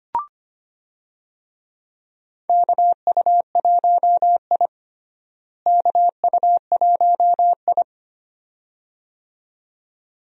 Morse code, heard as a keyed tone, about 25 wpm, 700 Hz. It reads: KU1I KU1I
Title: USA Call Signs - 4 Characters - Repeated 2x - 25wpm